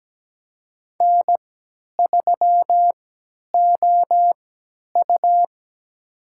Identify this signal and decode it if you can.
Morse code, heard as a keyed tone, about 17 wpm, spaced as for 10 wpm, 700 Hz. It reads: N3OU
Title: USA Call Signs - 4 Characters - 3x Character Spacing - 17wpm